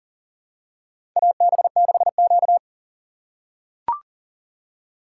Morse code, heard as a keyed tone, about 40 wpm, 700 Hz. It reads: AB6Q
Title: USA Call Signs - 2x Character Spacing - 40wpm